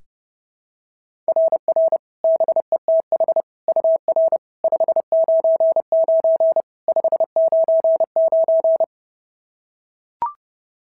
Morse code, heard as a keyed tone, about 26 wpm, 650 Hz. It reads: RR BETH UR 599 599
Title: SOTA QSO Practice 30wpm